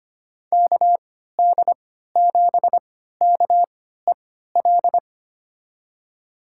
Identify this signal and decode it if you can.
Morse code, heard as a keyed tone, about 25 wpm, spaced as for 14 wpm, 700 Hz. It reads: KD7KEL
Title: USA Call Signs - 3x Character Spacing - 25wpm